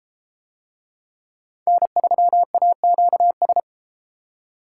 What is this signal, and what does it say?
Morse code, heard as a keyed tone, about 33 wpm, 700 Hz. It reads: N3AQS